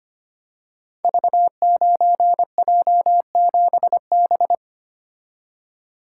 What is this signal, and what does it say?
Morse code, heard as a keyed tone, about 25 wpm, 700 Hz. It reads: V9J7B
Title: Code Groups - 5 Letter:Numbers - 25wpm